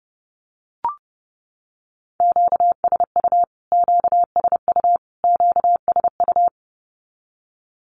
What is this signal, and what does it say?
Morse code, heard as a keyed tone, about 30 wpm, 700 Hz. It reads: QSU QSU QSU
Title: Q-Codes Repeated 3x 30wpm